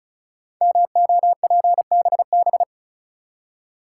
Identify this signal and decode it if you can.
Morse code, heard as a keyed tone, about 36 wpm, 700 Hz. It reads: MOPBB
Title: Code Groups - 5 Letters - 35wpm